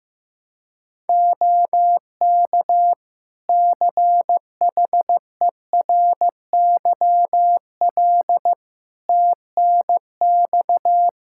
Morse code, heard as a keyed tone, about 15 wpm, 700 Hz. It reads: OK CHERYL TNX
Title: Practice QSO - Exchange 4 - Ending - 15wpm